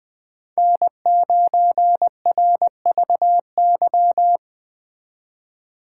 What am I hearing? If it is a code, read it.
Morse code, heard as a keyed tone, about 20 wpm, 700 Hz. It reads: N9RVY
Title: Speed Racing - USA 1x3 Call Signs 17wpm